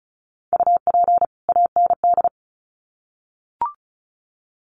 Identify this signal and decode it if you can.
Morse code, heard as a keyed tone, about 34 wpm, 700 Hz. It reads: UP AND
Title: Sets of 2 Words 35wpm